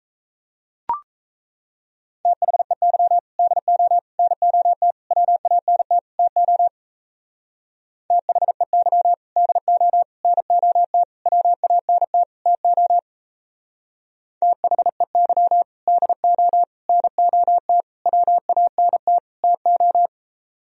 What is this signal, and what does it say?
Morse code, heard as a keyed tone, about 36 wpm, 700 Hz. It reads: THEY DO NOT WANT TO THEY DO NOT WANT TO THEY DO NOT WANT TO